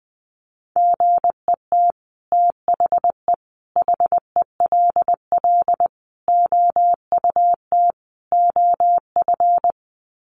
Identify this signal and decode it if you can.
Morse code, heard as a keyed tone, about 20 wpm, 700 Hz. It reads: GET THE HELL OUT OF
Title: Sets of 5 Words Encore2 20wpm